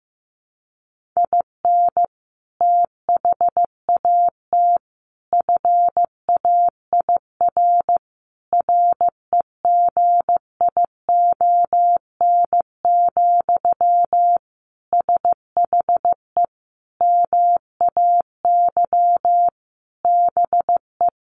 Morse code, heard as a keyed tone, about 14 wpm, 700 Hz. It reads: IN THAT FAIR REGION, SHE MAY BE
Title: Sentences from Top 1500 Words 15wpm